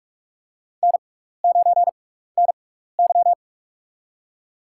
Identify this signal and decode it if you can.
Morse code, heard as a keyed tone, about 44 wpm, spaced as for 15 wpm, 700 Hz. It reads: N9NY
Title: USA Call Signs - 6x Character Spacing - 45wpm